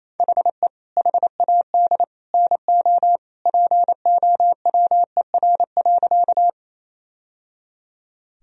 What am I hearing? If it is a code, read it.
Morse code, heard as a keyed tone, about 28 wpm, 700 Hz. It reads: HE HAD NO POWER.
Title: Sentences from Top 1000 Words - Review Part 2 - 28wpm